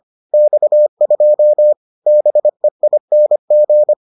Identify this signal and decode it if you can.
Morse code, heard as a keyed tone, about 25 wpm, 600 Hz. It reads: X2 BEING